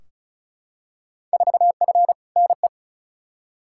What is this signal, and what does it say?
Morse code, heard as a keyed tone, about 36 wpm, 700 Hz. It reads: 4F NE
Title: Field Day Exchanges Encore - 35wpm